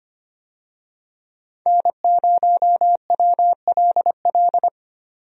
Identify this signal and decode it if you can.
Morse code, heard as a keyed tone, about 25 wpm, 700 Hz. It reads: N0WLL
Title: USA Call Signs - 5 Characters - 25wpm